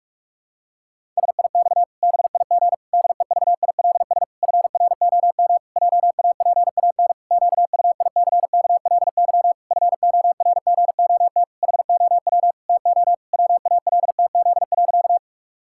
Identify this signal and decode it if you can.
Morse code, heard as a keyed tone, about 43 wpm, 700 Hz. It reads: SIX BIG DEVILS FROM JAPAN QUICKLY FORGOT HOW TO WALTZ.
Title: Pangram Challenge v2 - 45wpm